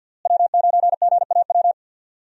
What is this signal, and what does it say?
Morse code, heard as a keyed tone, about 50 wpm, 700 Hz. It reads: W9GAW
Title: USA Call Signs - 5 Characters - Repeated 2x - 50wpm